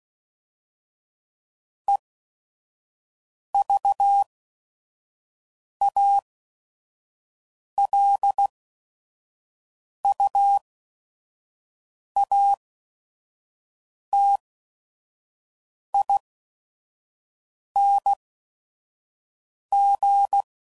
Morse code, heard as a keyed tone, about 16 wpm, spaced as for 5 wpm, 800 Hz. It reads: EVALUATING
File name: Prefixes / By First Letter / Es